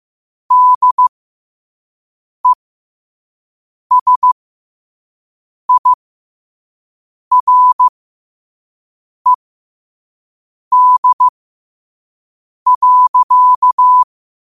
Morse code, heard as a keyed tone, about 15 wpm, spaced as for 5 wpm, 1000 Hz. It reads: DESIRED.
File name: qotdfiles / 5wpm